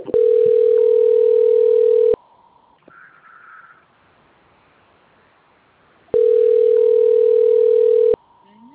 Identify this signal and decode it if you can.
Ringback tone (North America), 440 + 480 Hz.